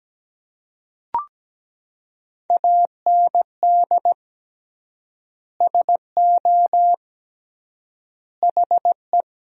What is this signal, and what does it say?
Morse code, heard as a keyed tone, about 17 wpm, 700 Hz. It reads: AND SO HE
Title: 3x Word-Spacing - Sentences from Top 500 Words - 17wpm